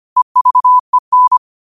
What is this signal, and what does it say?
Morse code, heard as a keyed tone, about 25 wpm, 1000 Hz. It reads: EVEN
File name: qotdfiles / 25wpm